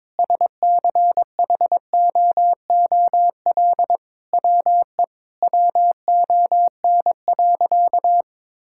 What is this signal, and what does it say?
Morse code, heard as a keyed tone, about 22 wpm, 700 Hz. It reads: SCHOOL WE WON.
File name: Sentences with 9-Word Phrases - 22wpm